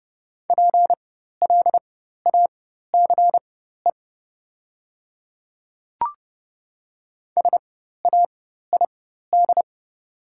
Morse code, heard as a keyed tone, about 30 wpm, spaced as for 14 wpm, 690 Hz. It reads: PLACE SAID
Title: 4x Character Spacing - Top 200 Words - 30wpm